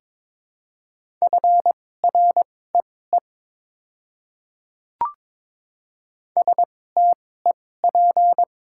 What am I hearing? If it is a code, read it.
Morse code, heard as a keyed tone, about 22 wpm, spaced as for 16 wpm, 700 Hz. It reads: FREE STEP
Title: 2x Character Spacing - Top 500 Words - 22wpm